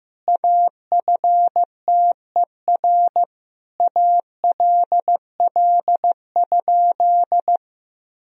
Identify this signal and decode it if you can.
Morse code, heard as a keyed tone, about 15 wpm, 700 Hz. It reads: AFTER ALL?
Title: Sentences from Top 1000 Words - Review Part 2 - 15wpm